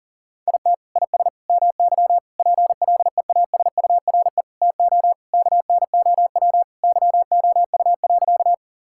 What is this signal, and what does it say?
Morse code, heard as a keyed tone, about 40 wpm, 700 Hz. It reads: IT IS MY PLEASURE TO KNOW YOU.
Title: Sentences from Top 1600 Words 40wpm